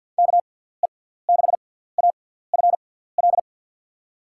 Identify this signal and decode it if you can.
Morse code, heard as a keyed tone, about 52 wpm, spaced as for 17 wpm, 700 Hz. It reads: KE6AFL
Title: USA Call Signs - 6x Character Spacing - 50wpm